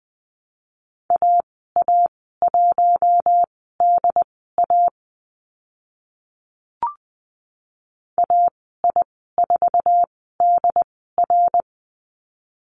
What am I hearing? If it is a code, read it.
Morse code, heard as a keyed tone, about 20 wpm, spaced as for 14 wpm, 700 Hz. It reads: AA1DA AI4DR
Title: 069 Lesson 12g - Letter d - Call Signs 20wpm